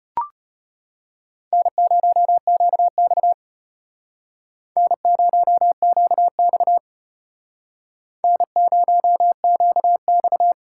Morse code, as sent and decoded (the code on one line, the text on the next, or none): -. ----- --.- -..- / -. ----- --.- -..- / -. ----- --.- -..-
N0QX N0QX N0QX